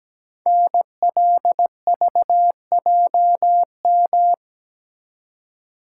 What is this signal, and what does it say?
Morse code, heard as a keyed tone, about 17 wpm, 700 Hz. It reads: NLVJM
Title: Code Groups - 5 Letters - 17wpm